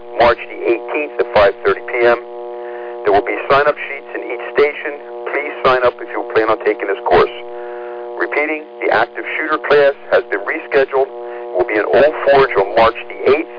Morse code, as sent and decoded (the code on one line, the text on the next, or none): none